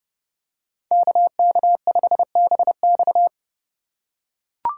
Morse code, heard as a keyed tone, about 29 wpm, 700 Hz. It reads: KK5BX